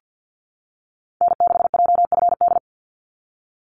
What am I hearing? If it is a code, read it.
Morse code, heard as a keyed tone, about 50 wpm, 700 Hz. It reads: N6JFD
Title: NA SOTA Activator Callsigns 50wpm